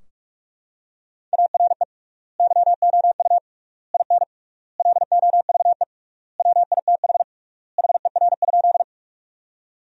Morse code, heard as a keyed tone, about 46 wpm, 700 Hz. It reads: ARE YOU IN LOVE WITH HER?
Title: 3x Word-Spacing - Sentences from Top 400 Words - 45wpm